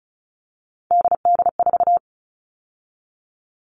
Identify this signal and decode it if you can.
Morse code, heard as a keyed tone, about 34 wpm, 700 Hz. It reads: DD4
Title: All Cut Numbers - 3-Digit 35wpm